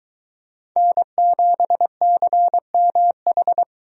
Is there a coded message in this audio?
Yes: Morse code from 0.8 to 3.6 s, heard as a keyed tone, about 21 wpm, 700 Hz. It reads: N7CMH